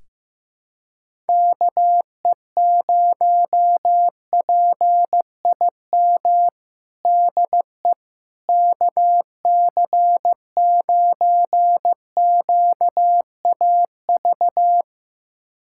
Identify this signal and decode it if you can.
Morse code, heard as a keyed tone, about 15 wpm, 700 Hz. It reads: KE0PIM DE KC9QAV
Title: USA Callsign DE Callsign - 6 Characters - 15wpm